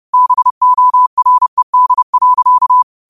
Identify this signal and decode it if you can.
Morse code, heard as a keyed tone, about 30 wpm, 1000 Hz. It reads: BORED.
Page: New quotes every day in morse code at 30 Words per minute.